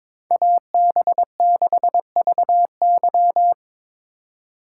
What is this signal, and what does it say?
Morse code, heard as a keyed tone, about 26 wpm, 700 Hz. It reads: AB6VY
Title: Speed Racing - USA 2x2 Call Signs 22wpm